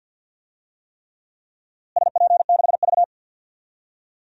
Morse code, heard as a keyed tone, about 47 wpm, 700 Hz. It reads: SP6V
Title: DXCC-100 Call Signs - 4 Characters - Repeated 3x - 50wpm